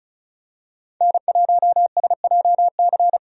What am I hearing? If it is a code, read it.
Morse code, heard as a keyed tone, about 36 wpm, 700 Hz. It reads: N1SJC
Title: USA Call Signs - 5 Characters - 35wpm